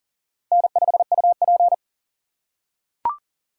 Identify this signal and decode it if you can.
Morse code, heard as a keyed tone, about 40 wpm, 700 Hz. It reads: N5UP
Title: USA Call Signs - 4 Characters - 6x Character Spacing - 40wpm